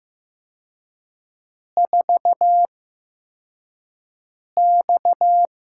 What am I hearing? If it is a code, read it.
Morse code, heard as a keyed tone, about 15 wpm, spaced as for 4 wpm, 700 Hz. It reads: 4X
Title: USA Call Signs - 4 Characters - 8x Character Spacing - 15wpm